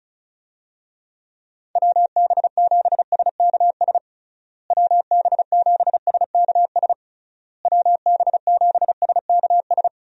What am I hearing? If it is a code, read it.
Morse code, heard as a keyed tone, about 36 wpm, 700 Hz. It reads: WB7SKS WB7SKS WB7SKS
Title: USA Call Signs - 6 Characters - Repeated 3x - 35wpm